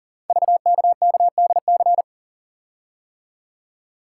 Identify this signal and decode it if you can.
Morse code, heard as a keyed tone, about 39 wpm, 700 Hz. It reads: VKKDC
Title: Code Groups - 5 Letters - 40wpm